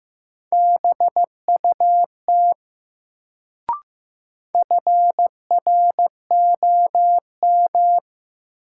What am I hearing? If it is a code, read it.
Morse code, heard as a keyed tone, about 15 wpm, 700 Hz. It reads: BUT FROM
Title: Sets of 2 Words - New Word Drill - 15wpm